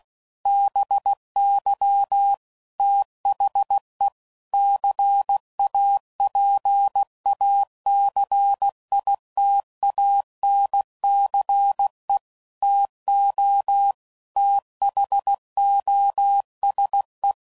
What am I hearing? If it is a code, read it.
Morse code, heard as a keyed tone, about 16 wpm, 800 Hz. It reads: BY THE CAPACITANCE TO THOSE
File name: CW Code Groups @15WPM / Technical Subjects / XC